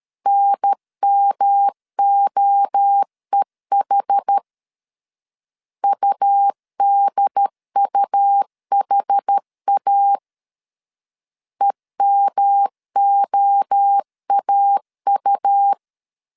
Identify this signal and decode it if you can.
Morse code, heard as a keyed tone, about 13 wpm, 800 Hz. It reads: NMOEH UDUHA EMOAU
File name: CW / DET / 1 3 1500C S C 12wpm 2750